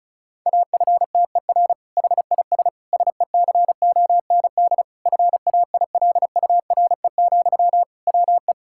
Morse code, heard as a keyed tone, about 34 wpm, 700 Hz. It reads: AFTER HIS SECOND FAILURE, WE
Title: Sentences from Top 1000 Words - Review Part 6 - 35wpm